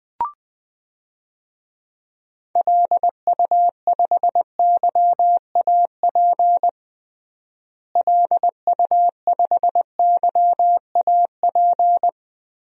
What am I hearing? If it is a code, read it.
Morse code, heard as a keyed tone, about 20 wpm, 700 Hz. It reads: LU5YAP LU5YAP
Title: DXCC-100 Call Signs - 6 Characters - Repeated 3x - 20wpm